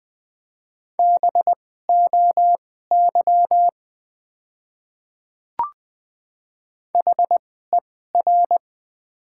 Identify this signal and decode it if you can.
Morse code, heard as a keyed tone, about 20 wpm, spaced as for 15 wpm, 700 Hz. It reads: BOY HER
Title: 2x Character Spacing - Top 200 Words - 20wpm